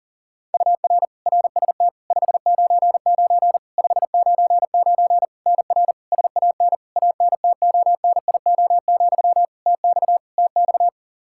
Morse code, heard as a keyed tone, about 40 wpm, 700 Hz. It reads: UR RST 599 599 NR SAN ANTONIO, TX TX
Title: K1USN SST QSO Practice 40wpm